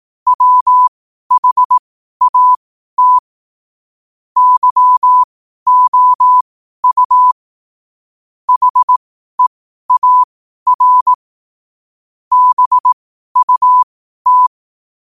Morse code, heard as a keyed tone, about 18 wpm, spaced as for 12 wpm, 1000 Hz. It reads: WHAT YOU HEAR BUT
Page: 12-wpm-morse-code-quotes